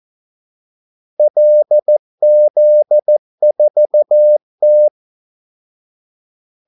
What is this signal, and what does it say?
Morse code, heard as a keyed tone, about 14 wpm, 600 Hz. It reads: LZ4T
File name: Callsigns-14wpm